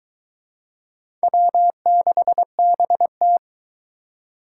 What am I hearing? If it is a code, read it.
Morse code, heard as a keyed tone, about 20 wpm, 700 Hz. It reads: W6BT